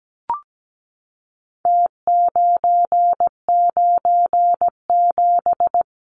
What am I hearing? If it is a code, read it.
Morse code, heard as a keyed tone, about 17 wpm, 700 Hz. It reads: T997
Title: Numbers - Contest Style - 4-Digit 17wpm